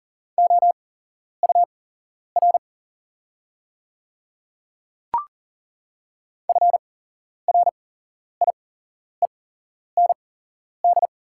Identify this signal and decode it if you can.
Morse code, heard as a keyed tone, about 40 wpm, spaced as for 11 wpm, 690 Hz. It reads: OUR FRIEND
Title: 8x Character Spacing - Top 300 Words - 40wpm